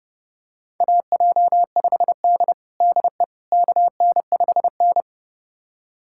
Morse code, heard as a keyed tone, about 30 wpm, 700 Hz. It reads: AJ5D DE KN5N
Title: USA Callsign DE Callsign - 4 Characters - 30wpm